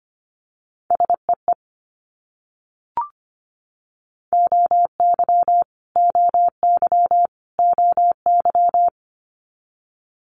Morse code, heard as a keyed tone, about 25 wpm, 700 Hz. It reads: SEE OY OY OY
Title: Multi-letter Phonemes Repeated 3x 25wpm